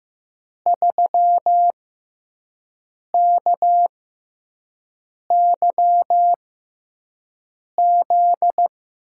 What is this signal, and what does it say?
Morse code, heard as a keyed tone, about 15 wpm, spaced as for 5 wpm, 700 Hz. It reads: 3KYZ
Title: USA Call Signs - 5 Characters - 6x Character Spacing - 15wpm